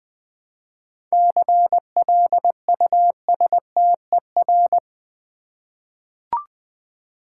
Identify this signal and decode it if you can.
Morse code, heard as a keyed tone, about 20 wpm, 700 Hz. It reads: CLUSTER
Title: 7 Letter Words - First 3 Letters Given - 20wpm